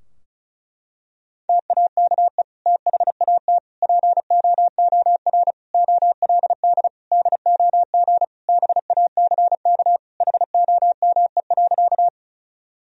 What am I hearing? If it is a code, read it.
Morse code, heard as a keyed tone, about 36 wpm, 700 Hz. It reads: TAKE THAT POOR OLD DOG BACK HOME.
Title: Sentences from Top 1000 Words - Review Part 2 - 35wpm